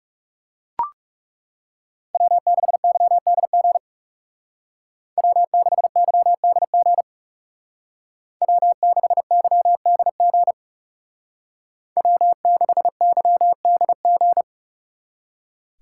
Morse code, heard as a keyed tone, about 37 wpm, 700 Hz. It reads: W6YDG W6YDG W6YDG W6YDG
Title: Speed Racing - USA 1x3 Call Signs 30wpm